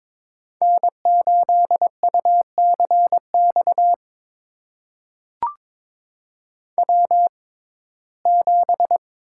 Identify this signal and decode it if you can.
Morse code, heard as a keyed tone, about 22 wpm, spaced as for 16 wpm, 700 Hz. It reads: N8UCX W7
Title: USA Call Signs - 5 Characters - 6x Character Spacing - 22wpm